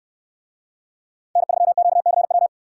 Morse code, heard as a keyed tone, about 65 wpm, 700 Hz. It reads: N3QCW